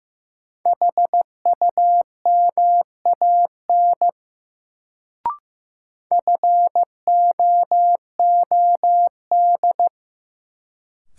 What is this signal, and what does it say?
Morse code, heard as a keyed tone, about 15 wpm, 700 Hz. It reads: HUMAN FOOD
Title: Sets of 5 Words - New Word Drill - 15wpm